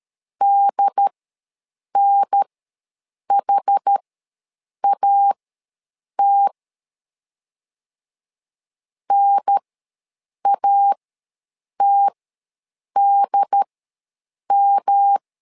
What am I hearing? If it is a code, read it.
Morse code, heard as a keyed tone, about 13 wpm, spaced as for 7 wpm, 800 Hz. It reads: DNHAT NATDM
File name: LEZIONI / 3 MWRO / 3 M ws 12wpm 1745